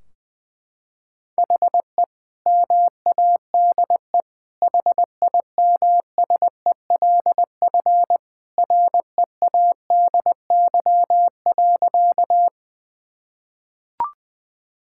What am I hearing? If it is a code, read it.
Morse code, heard as a keyed tone, about 20 wpm, 700 Hz. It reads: HE MADE HIMSELF READY.
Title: Sentences from Top 1600 Words 20wpm